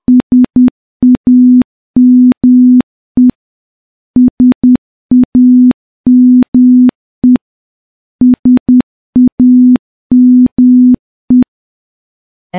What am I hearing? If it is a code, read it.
Morse code, heard as a keyed tone, about 10 wpm, 250 Hz. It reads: SAME SAME SAME
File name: morse-training-10WPM